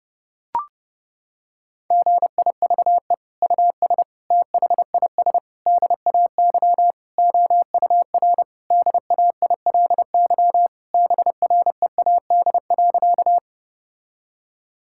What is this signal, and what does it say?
Morse code, heard as a keyed tone, about 32 wpm, 700 Hz. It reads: GIVE US THIS DAY OUR DAILY BREAD.
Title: Sentences from Top 1600 Words 30wpm